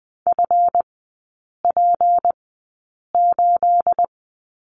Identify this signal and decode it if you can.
Morse code, heard as a keyed tone, about 20 wpm, spaced as for 8 wpm, 690 Hz. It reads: FP8
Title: 212 Lesson 34e - Number 8 - Three Characters 20wpm